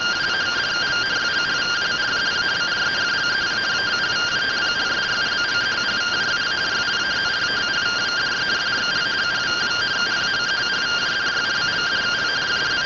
Two-tone teleprinter signal, about 170 Hz shift.
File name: findings / AXIAL fr130 3mm 409.gen-07 rtty rx